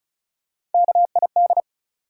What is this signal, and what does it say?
Morse code, heard as a keyed tone, about 36 wpm, 700 Hz. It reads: KID
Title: Sets of 5 Words - New Word Drill - 35wpm